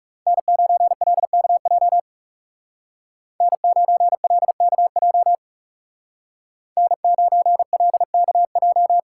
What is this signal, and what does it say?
Morse code, heard as a keyed tone, about 35 wpm, 700 Hz. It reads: N9LKJ N9LKJ N9LKJ
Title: Speed Racing - USA 1x3 Call Signs 30wpm